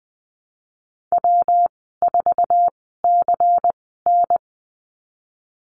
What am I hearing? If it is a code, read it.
Morse code, heard as a keyed tone, about 19 wpm, spaced as for 14 wpm, 700 Hz. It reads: W4CN
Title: 207 Lesson 33g - Letter j - Call Signs 20wpm